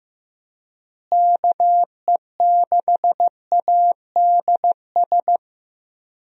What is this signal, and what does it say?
Morse code, heard as a keyed tone, about 15 wpm, 700 Hz. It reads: KE6ADS
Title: USA Call Signs - 6 Characters - Repeated 2x - 15wpm